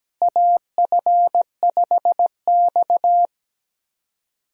Morse code, heard as a keyed tone, about 17 wpm, 700 Hz. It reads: AF5X
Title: USA Call Signs - 4 Characters - Repeated 3x - 17wpm